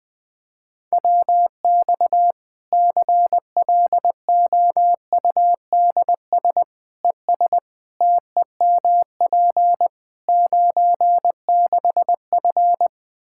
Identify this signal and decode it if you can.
Morse code, heard as a keyed tone, about 20 wpm, 700 Hz. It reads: WX CLOUDS ES TEMP 96F
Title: Practice QSO - Exchange 2 - Info - 20wpm